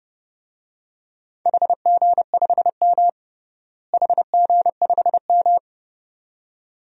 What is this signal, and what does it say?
Morse code, heard as a keyed tone, about 30 wpm, 700 Hz. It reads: HG5M HG5M
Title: DXCC-100 Call Signs - 4 Characters - Repeated 2x - 30wpm